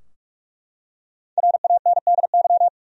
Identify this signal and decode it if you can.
Morse code, heard as a keyed tone, about 42 wpm, 700 Hz. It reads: RANDY
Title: USA Common Names - Max 5 Letters - 45wpm